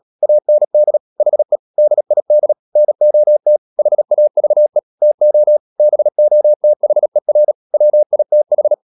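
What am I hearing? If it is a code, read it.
Morse code, heard as a keyed tone, about 37 wpm, 600 Hz. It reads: AND HE DID NOT HAVE TO BOTHER WITH
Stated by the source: Morse code practice at 35–39 wpm sending English text of Ken Follett's novel Never, from CW med Gnister.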